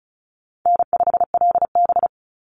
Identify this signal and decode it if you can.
Morse code, heard as a keyed tone, about 34 wpm, 700 Hz. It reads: N5LB
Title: Easy Call Signs Repeated 3x - 35wpm